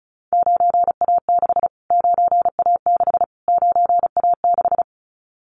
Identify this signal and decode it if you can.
Morse code, heard as a keyed tone, about 34 wpm, 700 Hz. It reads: 9A6 9A6 9A6
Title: Callsign Prefixes for Top 25 DXCC - Repeated 3x - 35wpm